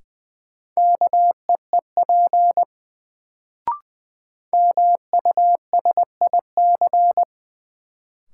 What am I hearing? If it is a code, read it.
Morse code, heard as a keyed tone, about 20 wpm, 700 Hz. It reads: KEEP MUSIC
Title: Sentences from Top 300 Words - New Word Drill - 20wpm